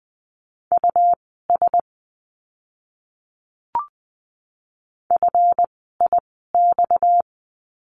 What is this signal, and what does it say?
Morse code, heard as a keyed tone, about 20 wpm, spaced as for 15 wpm, 700 Hz. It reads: US FIX
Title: 230 Lesson 37f - Letter x - Words 20wpm